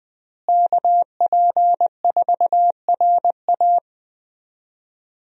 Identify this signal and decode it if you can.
Morse code, heard as a keyed tone, about 20 wpm, 700 Hz. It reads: KP4RA
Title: DXCC-100 Call Signs - 5 Characters - Repeated 3x - 20wpm